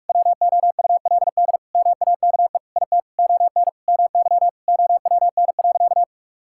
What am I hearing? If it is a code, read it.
Morse code, heard as a keyed tone, about 46 wpm, 700 Hz. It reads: WOULD MAKE IT ON MY OWN.
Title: Sentences from Top 1000 Words - Review Part 6 - 45wpm